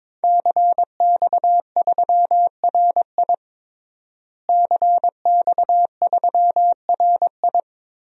Morse code, heard as a keyed tone, about 22 wpm, 700 Hz. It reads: CX3RI CX3RI
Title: DXCC-100 Call Signs - 5 Characters - Repeated 3x - 22wpm